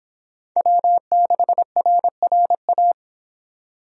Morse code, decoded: W6RRA

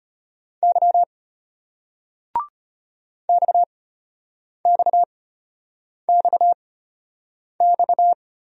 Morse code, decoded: Y XXXX

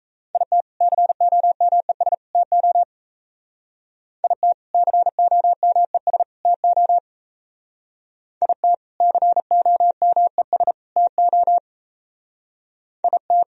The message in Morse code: IT COMES TO IT COMES TO IT COMES TO IT

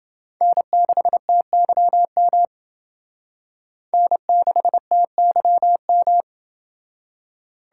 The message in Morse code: N6TYM N6TYM

